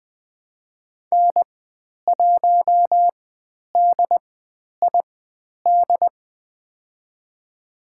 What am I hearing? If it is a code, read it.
Morse code, heard as a keyed tone, about 20 wpm, spaced as for 10 wpm, 700 Hz. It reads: N1DID
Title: 095 Lesson 16g - Letter c - Call Signs 20wpm